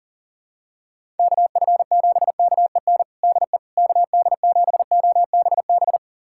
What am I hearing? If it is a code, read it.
Morse code, heard as a keyed tone, about 40 wpm, 700 Hz. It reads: KF7KEN DE KD7OBB
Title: USA Callsign DE Callsign - 6 Characters - 40wpm